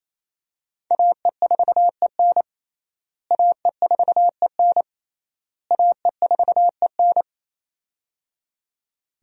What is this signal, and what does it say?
Morse code, heard as a keyed tone, about 28 wpm, 700 Hz. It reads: AE4EN AE4EN AE4EN
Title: USA Call Signs - 5 Characters - Repeated 3x - 28wpm